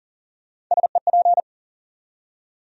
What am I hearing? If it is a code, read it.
Morse code, heard as a keyed tone, about 40 wpm, 700 Hz. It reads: SEP
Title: USA Call Sign Postfixes - 3 Letters 40wpm